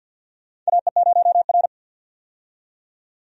Morse code, heard as a keyed tone, about 52 wpm, 700 Hz. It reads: AE0R
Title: USA Call Signs - 4 Characters - 50wpm